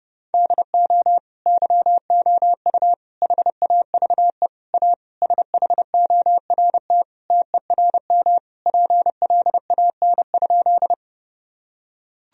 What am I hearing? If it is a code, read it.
Morse code, heard as a keyed tone, about 30 wpm, 700 Hz. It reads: DO YOU HAVE A SHORT TERM PLAN?
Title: Sentences from Top 1000 Words - Review Part 4 - 30wpm